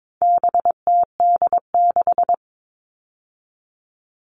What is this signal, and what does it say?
Morse code, heard as a keyed tone, about 22 wpm, 700 Hz. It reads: BTD6
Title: All Cut Numbers - 4-Digit 22wpm